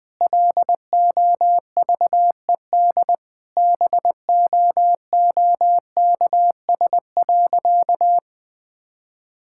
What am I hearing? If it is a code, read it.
Morse code, heard as a keyed tone, about 20 wpm, 700 Hz. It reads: LOVED BOOKS.
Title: Sentences with 3-Word Phrases - 20wpm